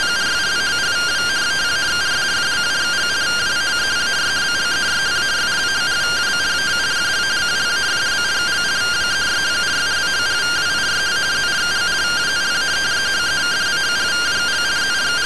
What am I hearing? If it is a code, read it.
Two-tone teleprinter signal, about 170 Hz shift.